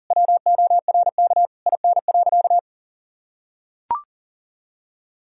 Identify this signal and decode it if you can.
Morse code, heard as a keyed tone, about 40 wpm, 700 Hz. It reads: WORK IN.
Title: Sentences from Top 1000 Words - Review Part 4 - 40wpm